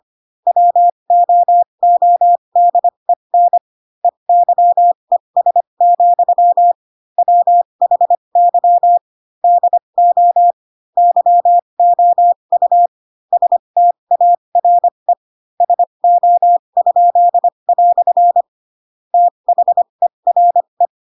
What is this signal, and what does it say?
Morse code, heard as a keyed tone, about 25 wpm, spaced as for 22 wpm, 700 Hz. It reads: WOODEN EYES, WHY DO YOU STARE SO?" THERE
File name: Pinocchio / Pinocchio ch3 18 25 ik1hge